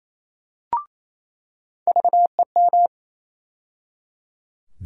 Morse code, heard as a keyed tone, about 28 wpm, 700 Hz. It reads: VEM
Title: USA Call Sign Postfixes - 3 Letters 28wpm